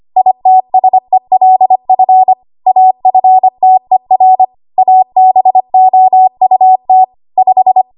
Morse code, heard as a keyed tone, about 25 wpm, 750 Hz. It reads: ITSELF AFTER ABOUT 5
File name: Archive / 25 WPM / 25WPM